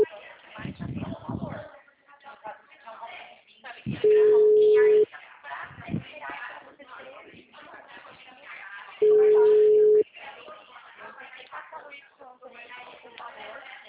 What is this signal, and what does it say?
Ringback tone (Europe, 425 Hz), with speech.
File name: out- -2009- - - .1328